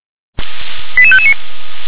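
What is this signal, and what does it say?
Five-tone selective call (ZVEI), heard as quick runs of five tones, tones of about 70 ms. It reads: 88559
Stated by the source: Received on an SDR.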